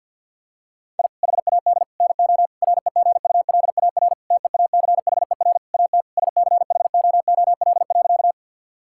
Morse code, heard as a keyed tone, about 49 wpm, 700 Hz. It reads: I HAD NO REGULAR TEACHER AT SCHOOL.
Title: Sentences from Top 1000 Words - Review Part 6 - 50wpm